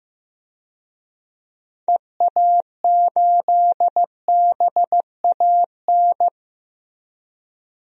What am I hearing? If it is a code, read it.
Morse code, heard as a keyed tone, about 15 wpm, 700 Hz. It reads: EA8BAN
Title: DXCC-100 Call Signs - 6 Characters - 15wpm